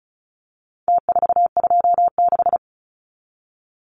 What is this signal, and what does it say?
Morse code, heard as a keyed tone, about 34 wpm, 700 Hz. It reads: T426
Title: Numbers - Contest Style - 4-Digit 35wpm